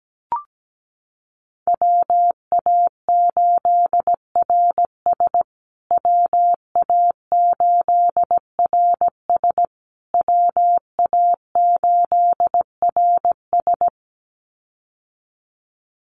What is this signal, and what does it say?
Morse code, heard as a keyed tone, about 17 wpm, 700 Hz. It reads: WA8RS WA8RS WA8RS
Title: USA Call Signs Repeated 3x 17wpm